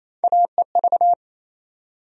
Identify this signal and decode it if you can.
Morse code, heard as a keyed tone, about 28 wpm, 700 Hz. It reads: AEV
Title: USA Call Sign Postfixes - 3 Letters 28wpm